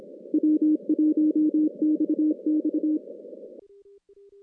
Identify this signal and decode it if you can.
Morse code, heard as a keyed tone, about 26 wpm, 310 Hz. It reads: W1XX
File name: contests / CW-SS-k5tr- -2100